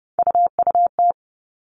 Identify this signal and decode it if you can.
Morse code, heard as a keyed tone, about 30 wpm, 700 Hz. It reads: UUT
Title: All Cut Numbers - 3-Digit 30wpm